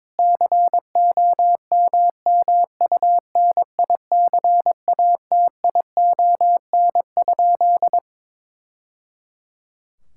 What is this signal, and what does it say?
Morse code, heard as a keyed tone, about 22 wpm, 700 Hz. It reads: COMMUNICATION?
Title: Sentences from Top 1000 Words - Review Part 6 - 22wpm